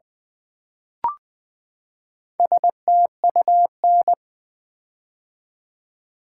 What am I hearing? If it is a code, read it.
Morse code, heard as a keyed tone, about 20 wpm, 700 Hz. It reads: STUN
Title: Build A Word - 4 Letters At A Time - 8-Letter Words - 20wpm